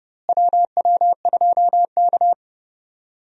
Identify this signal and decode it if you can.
Morse code, heard as a keyed tone, about 30 wpm, 700 Hz. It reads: WW2K
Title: USA Call Signs - 4 Characters - Repeated 3x - 30wpm